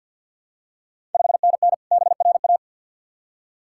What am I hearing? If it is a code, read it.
Morse code, heard as a keyed tone, about 51 wpm, 700 Hz. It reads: 5NN BRA
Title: Ontario QSO Party Exchange 50wpm